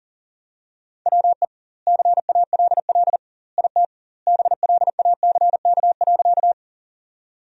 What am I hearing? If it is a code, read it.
Morse code, heard as a keyed tone, about 40 wpm, 700 Hz. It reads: WE CALL IT BLACK.
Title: 2x Word-Spacing - Sentences from Top 400 Words - 40wpm